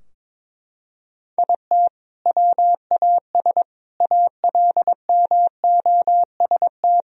Morse code, heard as a keyed tone, about 22 wpm, 700 Hz. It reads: IT WAS ALMOST
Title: Sentences from Top 1000 Words - Review Part 4 - 22wpm